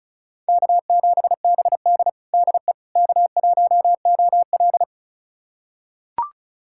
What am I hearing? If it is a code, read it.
Morse code, heard as a keyed tone, about 34 wpm, 700 Hz. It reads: K7BD DE K1OL
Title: USA Callsign DE Callsign - 4 Characters - 35wpm